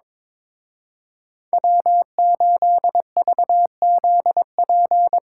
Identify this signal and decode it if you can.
Morse code, heard as a keyed tone, about 24 wpm, 700 Hz. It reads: W8VZP
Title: Speed Racing - USA 1x3 Call Signs 22wpm